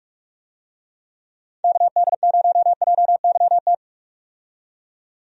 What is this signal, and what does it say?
Morse code, heard as a keyed tone, about 47 wpm, 700 Hz. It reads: KD0JYT
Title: USA Call Signs - 6 Characters - 45wpm